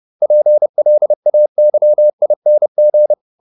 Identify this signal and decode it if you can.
Morse code, heard as a keyed tone, about 30 wpm, 600 Hz. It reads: PLAYING